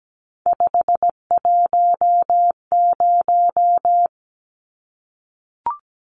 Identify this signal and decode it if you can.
Morse code, heard as a keyed tone, about 17 wpm, 700 Hz. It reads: 510
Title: Numbers - 3-Digit 17wpm